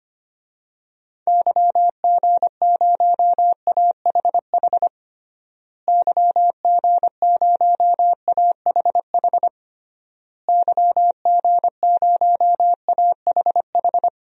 Morse code, heard as a keyed tone, about 25 wpm, 700 Hz. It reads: YG0AHH YG0AHH YG0AHH
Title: DXCC-100 Call Signs - 6 Characters - Repeated 3x - 25wpm